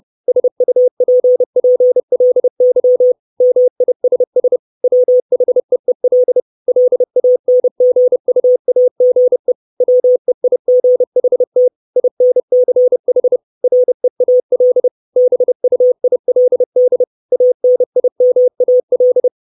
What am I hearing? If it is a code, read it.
Morse code, heard as a keyed tone, about 30 wpm, 500 Hz. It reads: SUPPLY MISS WHEEL LANGUAGE WEIGHT INCH REAL BUILD ANIMAL
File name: 632 Words 30WPM